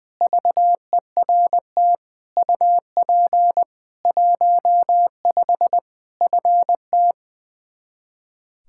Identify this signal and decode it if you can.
Morse code, heard as a keyed tone, about 21 wpm, 700 Hz. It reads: VERT UP 15 FT
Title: K1USN SST QSO Practice 20wpm